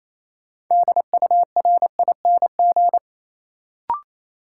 Morse code, heard as a keyed tone, about 28 wpm, 700 Hz. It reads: DURING
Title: Sentences from Top 700 Words - New Word Drill - 28wpm